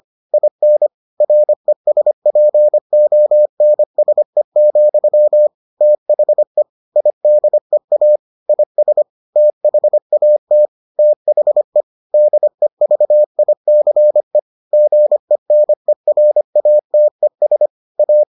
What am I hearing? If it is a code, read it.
Morse code, heard as a keyed tone, about 25 wpm, 600 Hz. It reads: IN RESPONSE, THE IDEA IS THAT THE DEVICE GENERATES A